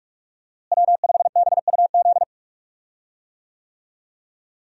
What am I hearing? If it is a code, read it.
Morse code, heard as a keyed tone, about 46 wpm, 700 Hz. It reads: W5BUZ
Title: USA Call Signs - 5 Characters - Repeated 2x - 45wpm